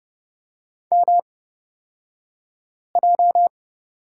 Morse code, heard as a keyed tone, about 30 wpm, spaced as for 5 wpm, 700 Hz. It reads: MJ
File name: Lesson 38d - Letter q - Two Characters 30wpm